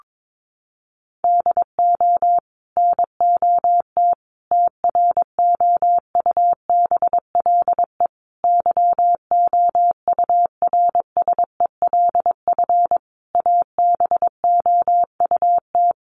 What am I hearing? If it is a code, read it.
Morse code, heard as a keyed tone, about 22 wpm, 700 Hz. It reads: DO NOT TROUBLE YOURSELF ABOUT
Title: Sentences from Top 1200 Words 22wpm